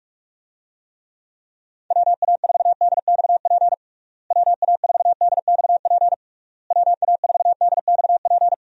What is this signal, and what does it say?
Morse code, heard as a keyed tone, about 46 wpm, 700 Hz. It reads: WA4DXP WA4DXP WA4DXP
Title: USA Call Signs - 6 Characters - Repeated 3x - 45wpm